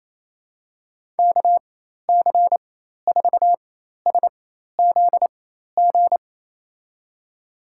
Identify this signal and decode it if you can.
Morse code, heard as a keyed tone, about 28 wpm, spaced as for 13 wpm, 700 Hz. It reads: KC4SZG
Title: USA Call Signs - 6 Characters - 4x Character Spacing - 28wpm